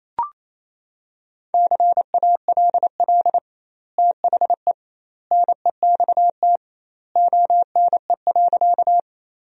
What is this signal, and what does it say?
Morse code, heard as a keyed tone, about 28 wpm, 700 Hz. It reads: CALL THE NEXT ONE.
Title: 2x Word-Spacing - Sentences from Top 300 Words - 28wpm